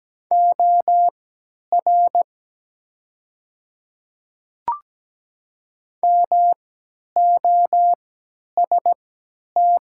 Morse code, heard as a keyed tone, about 17 wpm, spaced as for 10 wpm, 700 Hz. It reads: OR MOST
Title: 3x Character Spacing - Top 100 Words - 17wpm